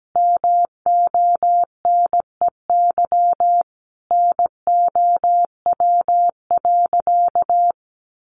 Morse code, heard as a keyed tone, about 17 wpm, 700 Hz. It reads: MONEY NOW.
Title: Sentences from Top 1300 Words 17wpm